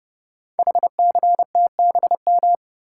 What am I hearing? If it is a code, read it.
Morse code, heard as a keyed tone, about 30 wpm, 700 Hz. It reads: HCTBM
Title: Code Groups - 5 Letters - 30wpm